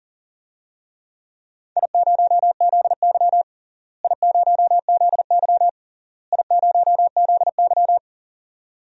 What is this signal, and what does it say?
Morse code, heard as a keyed tone, about 40 wpm, 700 Hz. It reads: I0ZY I0ZY I0ZY
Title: DXCC-100 Call Signs - 4 Characters - Repeated 3x - 40wpm